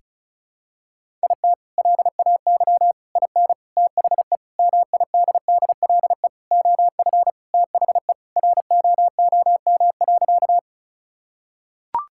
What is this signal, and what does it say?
Morse code, heard as a keyed tone, about 36 wpm, 700 Hz. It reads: IT LAY IN THE MIDDLE OF THE ROOM.
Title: Sentences from Top 1000 Words - Review Part 6 - 35wpm